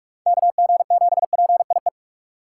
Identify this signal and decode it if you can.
Morse code, heard as a keyed tone, about 44 wpm, 700 Hz. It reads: KG7PIE